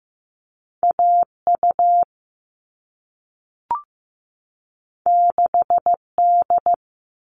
Morse code, heard as a keyed tone, about 15 wpm, 700 Hz. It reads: AU 6D
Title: All Cut Numbers - 2-Digit 15wpm